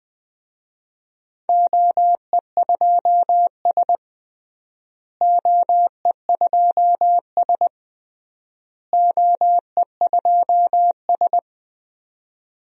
Morse code, heard as a keyed tone, about 20 wpm, 700 Hz. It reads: OE2S OE2S OE2S
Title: DXCC-100 Call Signs - 4 Characters - Repeated 3x - 20wpm